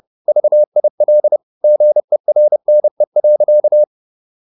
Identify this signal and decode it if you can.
Morse code, heard as a keyed tone, about 30 wpm, 600 Hz. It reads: VIL GERNE.
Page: Ferie 30 wpm